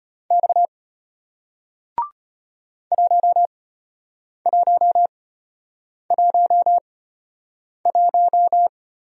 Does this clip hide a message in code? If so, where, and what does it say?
Yes: Morse code from 0.3 to 8.7 s, heard as a keyed tone, about 34 wpm, spaced as for 8 wpm, 700 Hz. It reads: X 1111